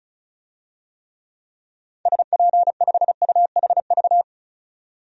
Morse code, heard as a keyed tone, about 36 wpm, 700 Hz. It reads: SP5UHV
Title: DXCC-100 Call Signs - 6 Characters - 35wpm